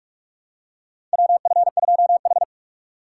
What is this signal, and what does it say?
Morse code, heard as a keyed tone, about 46 wpm, 700 Hz. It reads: WF2H
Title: USA Call Signs - 4 Characters - 4x Character Spacing - 45wpm